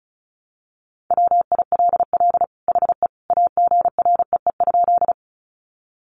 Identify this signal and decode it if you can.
Morse code, heard as a keyed tone, about 34 wpm, 700 Hz. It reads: WILL HE AGREE?
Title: Sentences from Top 1100 Words 35wpm